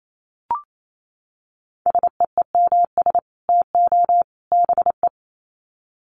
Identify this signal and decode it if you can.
Morse code, heard as a keyed tone, about 28 wpm, 700 Hz. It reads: SEEMS TO BE
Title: Sets of 3 Words 28wpm